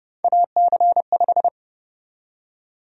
Morse code, heard as a keyed tone, about 31 wpm, 700 Hz. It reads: AC5
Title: USA Call Sign Prefixes - 30wpm